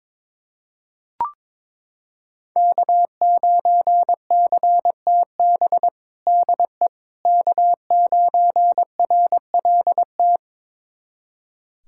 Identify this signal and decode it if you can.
Morse code, heard as a keyed tone, about 21 wpm, 700 Hz. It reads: K9CTB DE K9RLT
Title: USA Callsign DE Callsign - 5 Characters - 22wpm